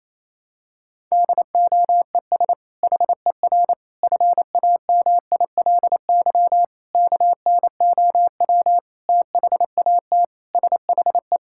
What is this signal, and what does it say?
Morse code, heard as a keyed tone, about 27 wpm, 700 Hz. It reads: DOES HER FAMILY KNOW THAT SHE
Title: Sentences from Top 1000 Words - Review Part 6 - 28wpm